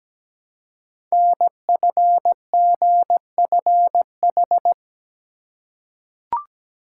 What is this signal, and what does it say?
Morse code, heard as a keyed tone, about 17 wpm, 700 Hz. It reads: NFGFH
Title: Code Groups - 5 Letters - 17wpm